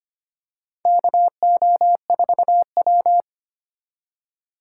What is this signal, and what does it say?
Morse code, heard as a keyed tone, about 25 wpm, 700 Hz. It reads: KO4W
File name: practice / 25wpm / Four Letter-Number Combinations - 25wpm